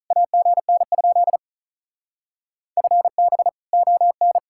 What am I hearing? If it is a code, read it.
Morse code, heard as a keyed tone, about 45 wpm, 700 Hz. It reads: AGN? FB ON